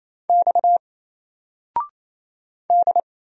Morse code, heard as a keyed tone, about 27 wpm, 700 Hz. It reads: X D
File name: Single-Letter-Number-28wpm